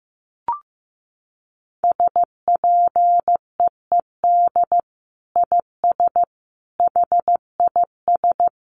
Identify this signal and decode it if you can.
Morse code, heard as a keyed tone, about 15 wpm, 700 Hz. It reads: SPEED IS HIS
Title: Sentences from Top 1300 Words 15wpm